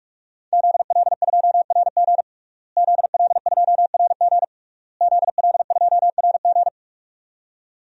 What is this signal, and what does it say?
Morse code, heard as a keyed tone, about 46 wpm, 700 Hz. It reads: ZL2RG ZL2RG ZL2RG
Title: DXCC-100 Call Signs - 5 Characters - Repeated 3x - 45wpm